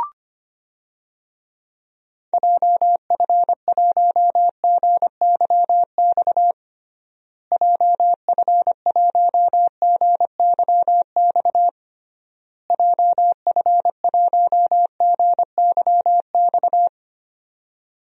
Morse code, heard as a keyed tone, about 25 wpm, 700 Hz. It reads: JF1GYX JF1GYX JF1GYX
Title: DXCC-100 Call Signs - 6 Characters - Repeated 3x - 25wpm